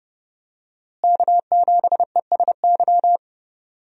Morse code, heard as a keyed tone, about 30 wpm, 700 Hz. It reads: K7ESY
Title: USA Call Signs - 5 Characters - Repeated 2x - 30wpm